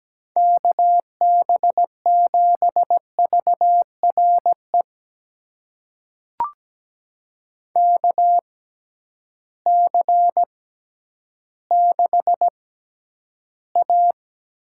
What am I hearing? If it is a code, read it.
Morse code, heard as a keyed tone, about 17 wpm, spaced as for 10 wpm, 700 Hz. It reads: KB7VRE KC6A
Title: USA Call Signs - 6 Characters - 6x Character Spacing - 17wpm